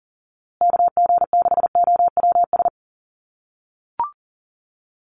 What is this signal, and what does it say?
Morse code, heard as a keyed tone, about 40 wpm, 700 Hz. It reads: KG6OWS
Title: USA Call Signs Repeated 3x 40wpm